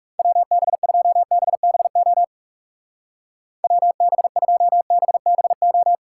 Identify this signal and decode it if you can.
Morse code, heard as a keyed tone, about 36 wpm, 700 Hz. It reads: WB2BBO WB2BBO
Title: Speed Racing - USA 2x3 Call Signs 30wpm